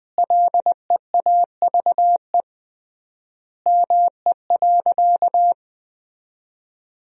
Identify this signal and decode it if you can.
Morse code, heard as a keyed tone, about 20 wpm, 700 Hz. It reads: LEAVE ME.
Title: 3x Word-Spacing - Sentences from Top 400 Words - 20wpm